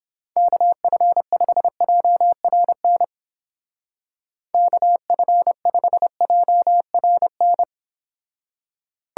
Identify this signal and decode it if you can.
Morse code, heard as a keyed tone, about 29 wpm, 700 Hz. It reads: KF5JRN KF5JRN